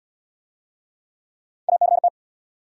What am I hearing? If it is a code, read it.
Morse code, heard as a keyed tone, about 71 wpm, 700 Hz. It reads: ACT